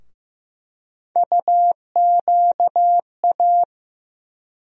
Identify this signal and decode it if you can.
Morse code, heard as a keyed tone, about 15 wpm, 700 Hz. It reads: UQA
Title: USA Call Sign Postfixes - 3 Letters 15wpm